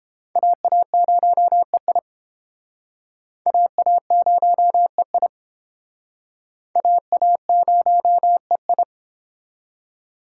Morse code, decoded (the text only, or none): AA0EI AA0EI AA0EI